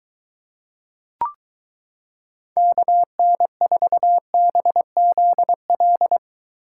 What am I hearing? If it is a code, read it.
Morse code, heard as a keyed tone, about 23 wpm, 700 Hz. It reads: KN4BZL